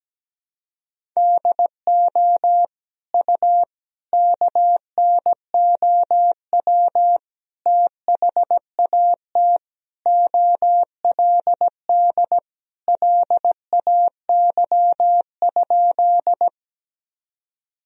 Morse code, heard as a keyed tone, about 17 wpm, 700 Hz. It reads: DO U KNOW THAT OLD LAY?
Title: Sentences from Top 400 Words and Abbreviations 17wpm